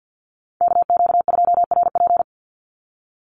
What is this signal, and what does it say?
Morse code, heard as a keyed tone, about 50 wpm, 700 Hz. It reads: KQ2RP
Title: NA SOTA Chaser Callsigns 50wpm